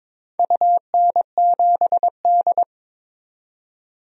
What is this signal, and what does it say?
Morse code, heard as a keyed tone, about 22 wpm, 700 Hz. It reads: UN7D
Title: DXCC-100 Call Signs - 4 Characters - 22wpm